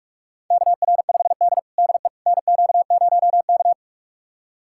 Morse code, heard as a keyed tone, about 42 wpm, 700 Hz. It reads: KR5D DE NQ0K
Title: USA Callsign DE Callsign - 4 Characters - 45wpm